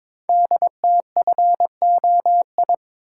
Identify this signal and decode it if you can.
Morse code, heard as a keyed tone, about 23 wpm, 700 Hz. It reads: DTFOI
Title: Code Groups - 5 Letters - 22wpm